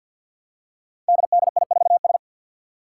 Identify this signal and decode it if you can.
Morse code, heard as a keyed tone, about 51 wpm, 700 Hz. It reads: DDI4S
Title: Five Challenging Letter-Number Combinations 50wpm